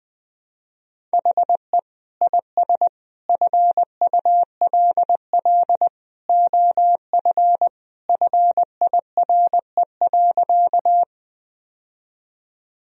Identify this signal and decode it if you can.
Morse code, heard as a keyed tone, about 20 wpm, 700 Hz. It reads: HE IS FULL OF FIRE.
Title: Sentences from Top 1000 Words - Review Part 3 - 20wpm